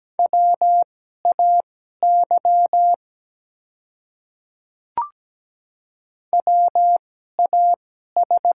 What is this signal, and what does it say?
Morse code, heard as a keyed tone, about 17 wpm, spaced as for 12 wpm, 700 Hz. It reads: WAY WAS
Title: 2x Character Spacing - Top 200 Words - 17wpm